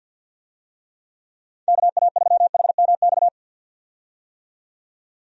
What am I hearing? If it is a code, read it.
Morse code, heard as a keyed tone, about 51 wpm, 700 Hz. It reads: KA3HMX
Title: USA Call Signs - 6 Characters - 50wpm